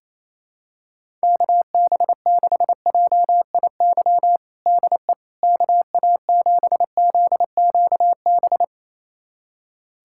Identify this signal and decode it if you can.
Morse code, heard as a keyed tone, about 28 wpm, 700 Hz. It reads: KB6JIY DE KA7ZQB
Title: USA Callsign DE Callsign - 6 Characters - 28wpm